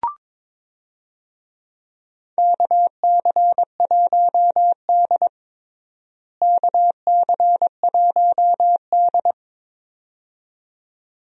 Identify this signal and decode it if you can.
Morse code, heard as a keyed tone, about 21 wpm, 700 Hz. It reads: KC1D KC1D
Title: USA Call Signs - 4 Characters - Repeated 2x - 22wpm